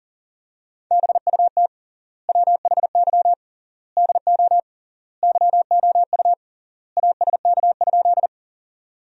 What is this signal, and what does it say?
Morse code, heard as a keyed tone, about 40 wpm, 700 Hz. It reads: BUT WHY DO YOU ASK?
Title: 3x Word-Spacing - Sentences from Top 200 Words - 40wpm